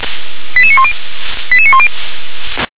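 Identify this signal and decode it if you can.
Five-tone selective call (ZVEI), heard as quick runs of five tones, tones of about 70 ms. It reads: 88910 88910
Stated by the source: ZVEI five-tone selective call received by SDR.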